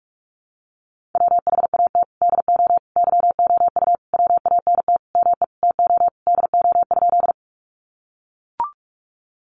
Morse code, heard as a keyed tone, about 44 wpm, 700 Hz. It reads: WHAT DO YOU WANT ME TO DO?
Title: Sentences from Top 200 Words 45wpm